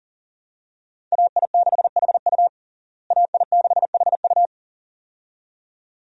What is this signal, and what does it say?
Morse code, heard as a keyed tone, about 40 wpm, 700 Hz. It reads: AI6HU AI6HU
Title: USA Call Signs - 5 Characters - Repeated 2x - 40wpm